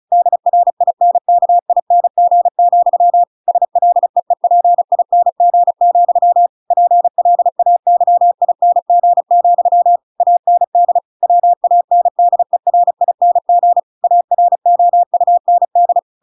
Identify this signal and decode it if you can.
Morse code, heard as a keyed tone, about 34 wpm, 700 Hz. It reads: DRINKING, SLEEPING, PLAYING, AND WANDERING AROUND